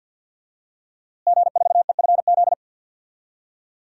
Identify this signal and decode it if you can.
Morse code, heard as a keyed tone, about 52 wpm, 700 Hz. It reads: G4EFZ
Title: DXCC-100 Call Signs - 5 Characters - Repeated 2x - 50wpm